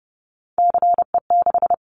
Morse code, heard as a keyed tone, about 30 wpm, 700 Hz. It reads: CE6
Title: Callsign Prefixes for Top 100 DXCC - 30wpm